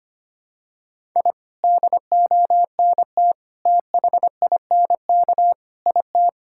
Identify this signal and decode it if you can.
Morse code, heard as a keyed tone, about 25 wpm, 700 Hz. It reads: I DONT THINK IT
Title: Sentences from Top 1000 Words - Review Part 6 - 25wpm